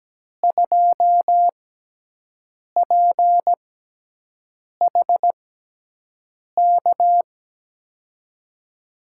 Morse code, heard as a keyed tone, about 17 wpm, spaced as for 6 wpm, 700 Hz. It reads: 2PHK
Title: USA Call Signs - 6 Characters - 6x Character Spacing - 17wpm